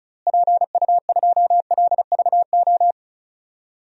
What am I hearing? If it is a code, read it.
Morse code, heard as a keyed tone, about 34 wpm, 700 Hz. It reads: PU2LVO
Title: DXCC-100 Call Signs - 6 Characters - Repeated 3x - 35wpm